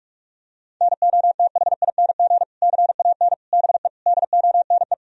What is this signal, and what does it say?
Morse code, heard as a keyed tone, about 41 wpm, 700 Hz. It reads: NOTHING CAN BE DONE
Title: Sentences from Top 1000 Words - Review Part 2 - 45wpm